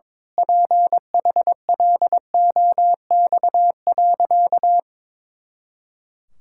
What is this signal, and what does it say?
Morse code, heard as a keyed tone, about 22 wpm, 700 Hz. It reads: PHLOX.
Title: Pangram Challenge v2 - 22wpm